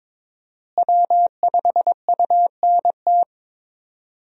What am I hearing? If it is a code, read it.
Morse code, heard as a keyed tone, about 22 wpm, 700 Hz. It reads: W5UNT
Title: USA Call Signs - 5 Characters - Repeated 3x - 22wpm